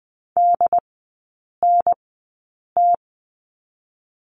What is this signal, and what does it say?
Morse code, heard as a keyed tone, about 20 wpm, spaced as for 8 wpm, 690 Hz. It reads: DNT
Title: 093 Lesson 16e - Letter c - Three Characters 20wpm